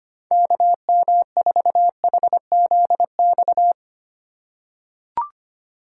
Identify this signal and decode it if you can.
Morse code, heard as a keyed tone, about 25 wpm, 700 Hz. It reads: KM4HZX
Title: USA Call Signs - 6 Characters - 3x Character Spacing - 25wpm